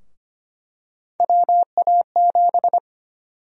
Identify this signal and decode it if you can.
Morse code, heard as a keyed tone, about 25 wpm, 700 Hz. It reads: WA7
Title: USA Call Sign Prefixes - Repeated 2x - 25wpm